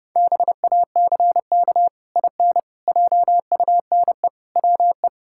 Morse code, heard as a keyed tone, about 30 wpm, 700 Hz. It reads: BACK IN JUNE WE
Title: Pangram Challenge v2 - 30wpm